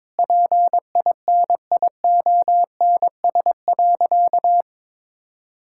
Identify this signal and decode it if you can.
Morse code, heard as a keyed tone, about 22 wpm, 700 Hz. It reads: PINIONS.
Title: Pangram Challenge v2 - 22wpm